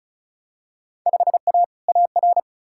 Morse code, heard as a keyed tone, about 36 wpm, 700 Hz. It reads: 5A AR
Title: Field Day Exchanges Encore - 35wpm